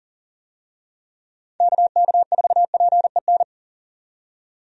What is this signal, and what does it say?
Morse code, heard as a keyed tone, about 38 wpm, 700 Hz. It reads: KK4PEN
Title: USA Call Signs - 6 Characters - 40wpm